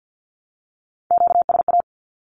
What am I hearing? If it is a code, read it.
Morse code, heard as a keyed tone, about 50 wpm, 700 Hz. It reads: QSA
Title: Q-Codes 50wpm